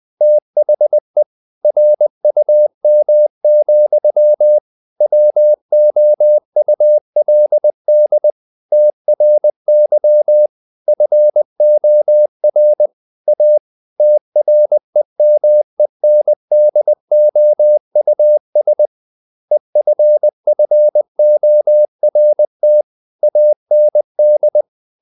Morse code, heard as a keyed tone, about 20 wpm, 600 Hz. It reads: THE RUM, WOULD TRY FOR A TREMENDOUS EFFORT AND
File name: the old man and the sea part 2 20wpm